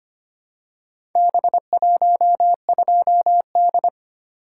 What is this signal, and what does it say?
Morse code, heard as a keyed tone, about 25 wpm, 700 Hz. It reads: B12D
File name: practice / 25wpm / Four Letter-Number Combinations - 25wpm